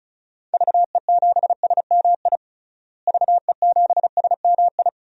Morse code, heard as a keyed tone, about 36 wpm, 700 Hz. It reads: VE7SMI VE7SMI
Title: DXCC-100 Call Signs - 6 Characters - Repeated 2x - 35wpm